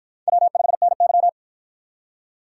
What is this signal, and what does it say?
Morse code, heard as a keyed tone, about 49 wpm, 700 Hz. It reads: W5NY